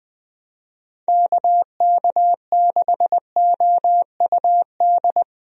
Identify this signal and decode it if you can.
Morse code, heard as a keyed tone, about 21 wpm, 700 Hz. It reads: KK6OUD
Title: USA Call Signs - 6 Characters - 20wpm